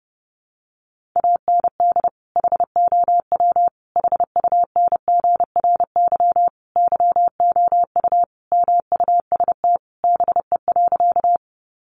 Morse code, heard as a keyed tone, about 30 wpm, 700 Hz. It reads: AND HOW HUNGRY YOU MUST BE.
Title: Sentences from Top 1400 Words 30wpm